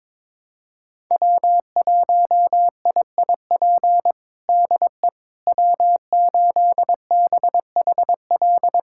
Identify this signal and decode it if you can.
Morse code, heard as a keyed tone, about 22 wpm, 700 Hz. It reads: W1IIP DE W8BHL
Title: USA Callsign DE Callsign - 5 Characters - 22wpm